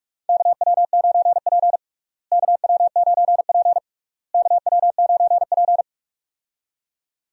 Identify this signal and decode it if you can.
Morse code, heard as a keyed tone, about 42 wpm, 700 Hz. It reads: KW9P KW9P KW9P
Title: USA Call Signs - 4 Characters - Repeated 3x - 45wpm